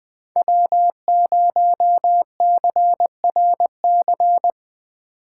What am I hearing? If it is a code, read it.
Morse code, heard as a keyed tone, about 23 wpm, 700 Hz. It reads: W0CRC
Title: Speed Racing - USA 1x3 Call Signs 20wpm